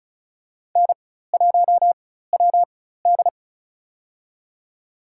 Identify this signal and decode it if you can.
Morse code, heard as a keyed tone, about 36 wpm, spaced as for 16 wpm, 700 Hz. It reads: N1WD
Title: USA Call Signs - 4x Character Spacing - 35wpm